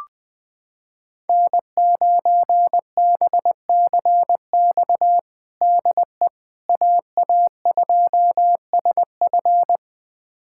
Morse code, heard as a keyed tone, about 20 wpm, 700 Hz. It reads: N9BCX DE AA2SF
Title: USA Callsign DE Callsign - 5 Characters - 20wpm